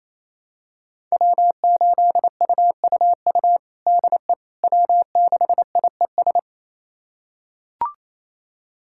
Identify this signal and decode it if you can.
Morse code, heard as a keyed tone, about 28 wpm, 700 Hz. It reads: W8UUU DE W6IES
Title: USA Callsign DE Callsign - 5 Characters - 28wpm